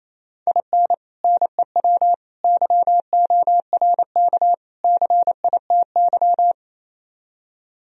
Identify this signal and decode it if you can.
Morse code, heard as a keyed tone, about 27 wpm, 700 Hz. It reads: IN NEW YORK CITY
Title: Speed-Racing - Sets of 4 Words 28wpm